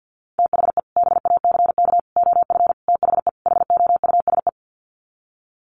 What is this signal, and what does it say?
Morse code, heard as a keyed tone, about 50 wpm, 700 Hz. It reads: THE BACK OF THE HOUSE
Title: Sets of 5 Words Encore2 50wpm